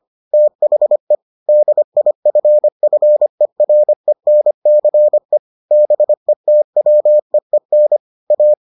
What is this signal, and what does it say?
Morse code, heard as a keyed tone, about 25 wpm, 600 Hz. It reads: THE DIFFERENCE BETWEEN A